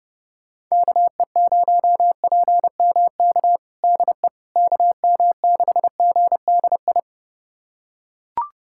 Morse code, heard as a keyed tone, about 30 wpm, 700 Hz. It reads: KE0PMK DE KM6GDI
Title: USA Callsign DE Callsign - 6 Characters - 30wpm